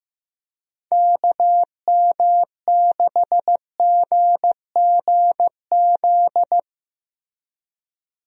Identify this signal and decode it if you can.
Morse code, heard as a keyed tone, about 15 wpm, 700 Hz. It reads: KM6GGZ
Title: USA Call Signs - 6 Characters - Repeated 2x - 15wpm